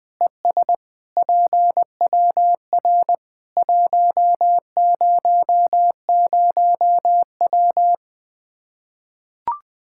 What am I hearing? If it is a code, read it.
Morse code, heard as a keyed tone, about 20 wpm, 700 Hz. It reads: ES PWR 100W
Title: Practice QSO - Exchange 2 - Info - 20wpm